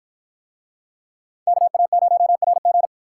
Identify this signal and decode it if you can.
Morse code, heard as a keyed tone, about 53 wpm, 700 Hz. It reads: KA0RG